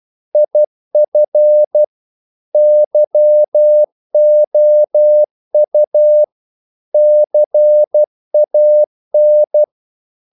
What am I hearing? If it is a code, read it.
Morse code, heard as a keyed tone, about 12 wpm, 600 Hz. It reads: IF YOU CAN
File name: CW / Practice / If12